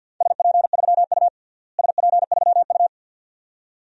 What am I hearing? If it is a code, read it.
Morse code, heard as a keyed tone, about 50 wpm, 700 Hz. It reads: SP3U SP3U
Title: DXCC-100 Call Signs - 4 Characters - Repeated 2x - 50wpm